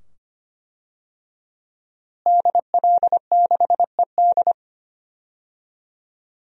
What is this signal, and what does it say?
Morse code, heard as a keyed tone, about 25 wpm, 700 Hz. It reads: DL6ED
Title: DXCC-100 Call Signs - 5 Characters - Repeated 2x - 25wpm